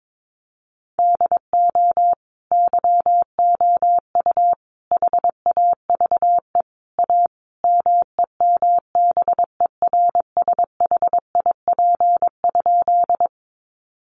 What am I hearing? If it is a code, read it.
Morse code, heard as a keyed tone, about 22 wpm, 700 Hz. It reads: DO YOU HAVE A MEMBERSHIP?
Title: Sentences from Top 1400 Words 22wpm